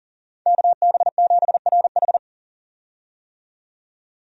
Morse code, heard as a keyed tone, about 40 wpm, 700 Hz. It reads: KB7RH
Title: USA Call Signs - 5 Characters - 40wpm